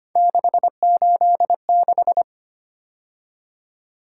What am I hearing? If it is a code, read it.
Morse code, heard as a keyed tone, about 25 wpm, 700 Hz. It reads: 686
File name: Three Letter-Number Combinations - 25wpm